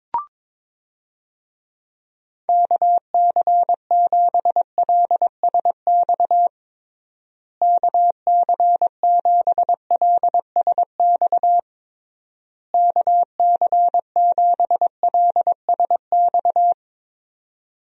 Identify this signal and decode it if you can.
Morse code, heard as a keyed tone, about 22 wpm, 700 Hz. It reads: KC7LSX KC7LSX KC7LSX
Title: USA Call Signs - 6 Characters - Repeated 3x - 22wpm